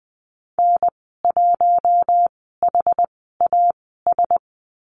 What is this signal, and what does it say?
Morse code, heard as a keyed tone, about 20 wpm, spaced as for 14 wpm, 700 Hz. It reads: N1HAS
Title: 069 Lesson 12g - Letter d - Call Signs 20wpm